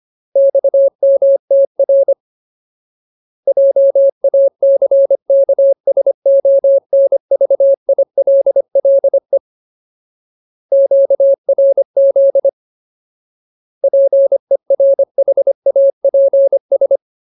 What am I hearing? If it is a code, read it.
Morse code, heard as a keyed tone, about 25 wpm, 550 Hz. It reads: XMTR JACKSONVILLE QRZ PERHAPS
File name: Ultimate Word List / 25WPM Ultimate